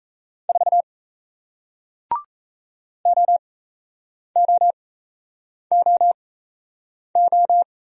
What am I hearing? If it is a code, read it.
Morse code, heard as a keyed tone, about 38 wpm, spaced as for 8 wpm, 700 Hz. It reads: 4 OOOO